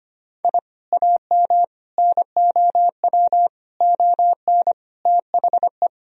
Morse code, heard as a keyed tone, about 25 wpm, 700 Hz. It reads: I AM NOW ON THE
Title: Sentences from Top 1000 Words - Review Part 2 - 25wpm